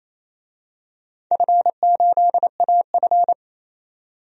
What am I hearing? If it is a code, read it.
Morse code, heard as a keyed tone, about 27 wpm, 700 Hz. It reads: F8AF
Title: DXCC-100 Call Signs - 4 Characters - 28wpm